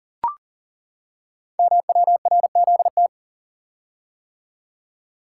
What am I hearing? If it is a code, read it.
Morse code, heard as a keyed tone, about 39 wpm, 690 Hz. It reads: MWRZT
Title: Code Groups - 5 Letters - 40wpm